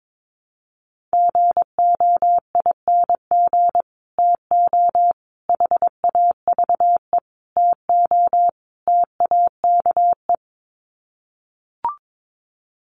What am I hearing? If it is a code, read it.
Morse code, heard as a keyed tone, about 22 wpm, 700 Hz. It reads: GOING TO HAVE TO TAKE
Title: Sets of 5 Words Encore 22wpm